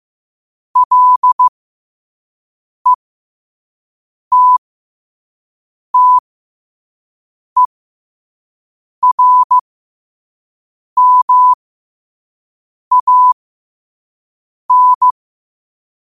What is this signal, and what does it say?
Morse code, heard as a keyed tone, about 15 wpm, spaced as for 5 wpm, 1000 Hz. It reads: LETTERMAN
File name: qotdfiles / 5wpm